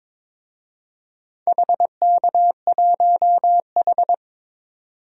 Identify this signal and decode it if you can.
Morse code, heard as a keyed tone, about 21 wpm, 700 Hz. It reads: HK1H
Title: DXCC-100 Call Signs - 4 Characters - 22wpm